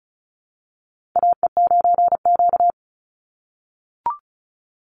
Morse code, heard as a keyed tone, about 34 wpm, 700 Hz. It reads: AE9Q
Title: NA SOTA Chaser Callsigns 35wpm